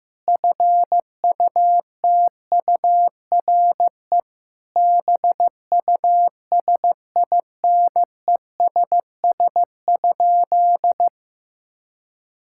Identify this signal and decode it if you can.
Morse code, heard as a keyed tone, about 15 wpm, 700 Hz. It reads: FUTURE BUSINESS?
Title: Sentences from Top 1000 Words - Review Part 6 - 15wpm